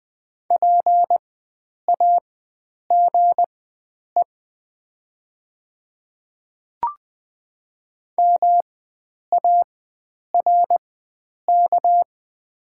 Morse code, heard as a keyed tone, about 20 wpm, spaced as for 9 wpm, 690 Hz. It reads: PAGE MARK
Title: 4x Character Spacing - Top 300 Words - 20wpm